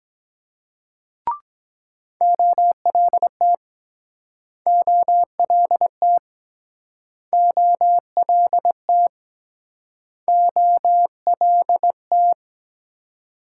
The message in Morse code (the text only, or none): OLT OLT OLT OLT